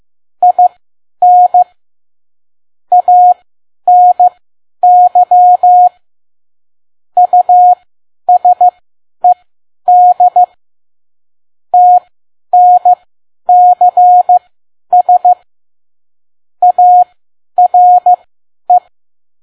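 Morse code, heard as a keyed tone, about 15 wpm, spaced as for 10 wpm, 720 Hz. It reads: IN ANY USED TNCS ARE
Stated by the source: Morse code practice at 10 WPM.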